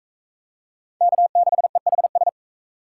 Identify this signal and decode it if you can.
Morse code, heard as a keyed tone, about 39 wpm, 700 Hz. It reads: K6EHS